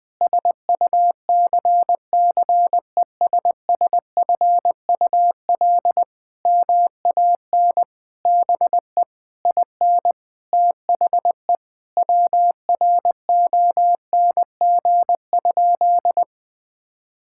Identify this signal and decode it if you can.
Morse code, heard as a keyed tone, about 20 wpm, 700 Hz. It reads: SUCCESSFUL MAN BE IN THE WRONG?
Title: Sentences from Top 1000 Words - Review Part 6 - 20wpm